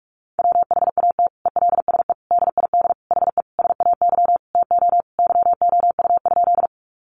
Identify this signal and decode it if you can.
Morse code, heard as a keyed tone, about 44 wpm, 700 Hz. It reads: WHAT ELSE DID HE SAY TO YOU?
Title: Sentences from Top 800 Words 45wpm